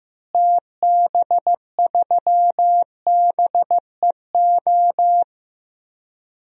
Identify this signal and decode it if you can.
Morse code, heard as a keyed tone, about 15 wpm, 700 Hz. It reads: TB3BEO
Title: DXCC-100 Call Signs - 6 Characters - Repeated 2x - 15wpm